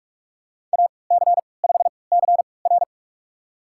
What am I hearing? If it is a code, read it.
Morse code, heard as a keyed tone, about 46 wpm, spaced as for 25 wpm, 700 Hz. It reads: AC5CR
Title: USA Call Signs - 3x Character Spacing - 45wpm